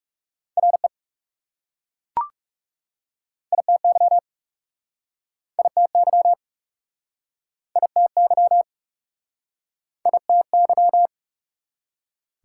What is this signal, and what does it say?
Morse code, heard as a keyed tone, about 37 wpm, 700 Hz. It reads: RE ITY ITY ITY ITY
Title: Speed-Racing Prefixes and Suffixes 30wpm